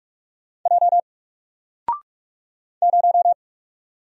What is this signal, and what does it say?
Morse code, heard as a keyed tone, about 44 wpm, 700 Hz. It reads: J 0
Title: Single Letter-Number Alt 45wpm